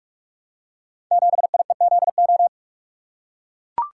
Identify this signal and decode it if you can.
Morse code, heard as a keyed tone, about 44 wpm, 700 Hz. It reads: 7IEZO
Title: Code Groups - 5 Letter:Numbers - 45wpm